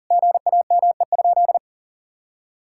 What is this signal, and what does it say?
Morse code, heard as a keyed tone, about 37 wpm, 700 Hz. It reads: GAME?
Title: Sentences from Top 1600 Words 40wpm